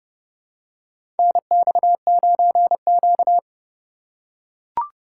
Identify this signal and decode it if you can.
Morse code, heard as a keyed tone, about 30 wpm, 700 Hz. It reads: NX9Q
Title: USA Call Signs - 4 Characters - 4x Character Spacing - 30wpm